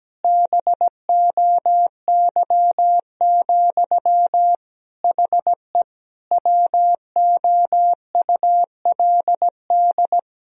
Morse code, heard as a keyed tone, about 17 wpm, 700 Hz. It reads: BOY, HE WOULD
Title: Sentences from Top 1000 Words - Review Part 3 - 17wpm